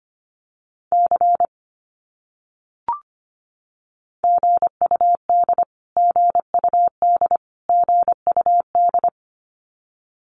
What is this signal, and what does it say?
Morse code, heard as a keyed tone, about 25 wpm, 700 Hz. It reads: C GUD GUD GUD
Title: Abbreviations Repeated 3x - 25wpm